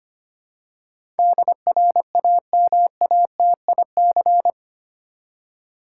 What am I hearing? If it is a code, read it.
Morse code, heard as a keyed tone, about 25 wpm, 700 Hz. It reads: DRAMATIC
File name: Eight Letter Words - 25wpm